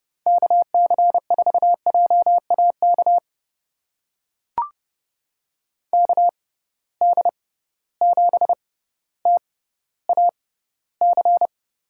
Morse code, heard as a keyed tone, about 30 wpm, spaced as for 15 wpm, 700 Hz. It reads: KC4JAK KD7TAC
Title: USA Call Signs - 6 Characters - 6x Character Spacing - 30wpm